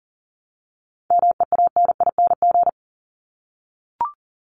Morse code, heard as a keyed tone, about 40 wpm, 700 Hz. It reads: MEANING
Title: ING Suffix - Top 100 Words with Base Words 40wpm